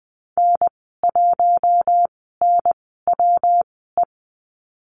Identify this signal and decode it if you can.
Morse code, heard as a keyed tone, about 20 wpm, spaced as for 15 wpm, 700 Hz. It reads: N1NWE
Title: 167 Lesson 27g - Number 7 - Call Signs 20wpm